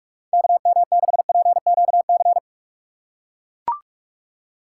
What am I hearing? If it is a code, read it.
Morse code, heard as a keyed tone, about 44 wpm, 700 Hz. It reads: KM6PQC
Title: USA Call Signs - 4x Character Spacing - 45wpm